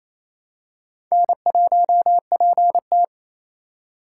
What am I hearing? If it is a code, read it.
Morse code, heard as a keyed tone, about 28 wpm, 700 Hz. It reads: N1PT
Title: USA Call Signs - 4x Character Spacing - 28wpm